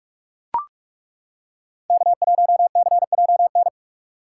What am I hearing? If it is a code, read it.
Morse code, heard as a keyed tone, about 39 wpm, 700 Hz. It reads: K1CJN